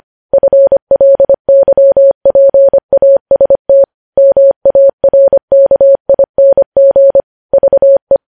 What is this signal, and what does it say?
Morse code, heard as a keyed tone, about 25 wpm, 550 Hz. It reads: FLYPAST MARKING VE
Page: News Headlines in Morse Code at 25 WPM